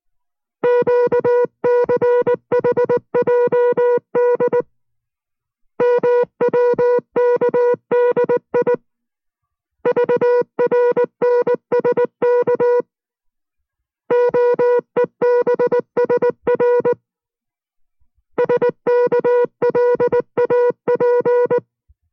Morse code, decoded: QCHJD MWKDI VRNSK OEBSR SKLAP